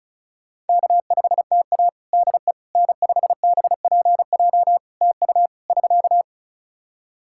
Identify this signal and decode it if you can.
Morse code, heard as a keyed tone, about 34 wpm, 700 Hz. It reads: K5TA DE N5BPJ TU <SK>
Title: K1USN SST QSO Practice 35wpm